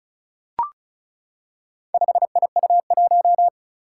Morse code, heard as a keyed tone, about 36 wpm, 700 Hz. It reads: 5IU1
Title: Four Challenging Letter-Number Combinations 35wpm